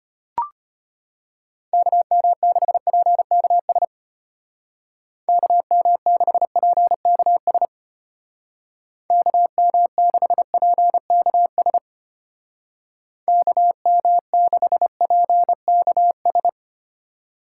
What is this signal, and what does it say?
Morse code, heard as a keyed tone, about 30 wpm, 700 Hz. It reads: KM6PKS KM6PKS KM6PKS KM6PKS
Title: Speed Racing - USA Call Signs 25wpm